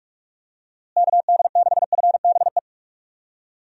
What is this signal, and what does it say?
Morse code, heard as a keyed tone, about 46 wpm, 700 Hz. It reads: KD6FBE
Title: USA Call Signs - 6 Characters - Repeated 2x - 45wpm